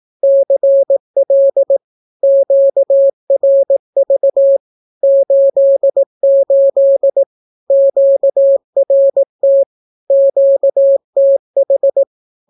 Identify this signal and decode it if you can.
Morse code, heard as a keyed tone, about 18 wpm, 550 Hz. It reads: CL QRV 88 QRT QTH
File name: naukapodstaw / skroty / skroty